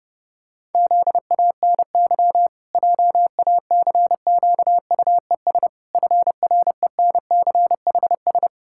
Morse code, heard as a keyed tone, about 30 wpm, 700 Hz. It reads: ZANY JACQUES FRENCHS
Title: Pangram Challenge v2 - 30wpm